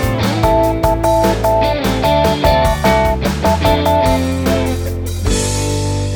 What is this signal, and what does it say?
Morse code, heard as a keyed tone, about 12 wpm, 660 Hz, with noise. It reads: YOU E